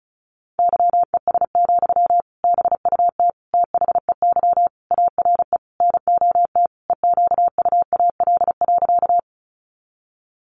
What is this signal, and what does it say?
Morse code, heard as a keyed tone, about 34 wpm, 700 Hz. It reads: YES, BUT THEY ARE NOT EQUAL.
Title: Sentences from Top 1400 Words 35wpm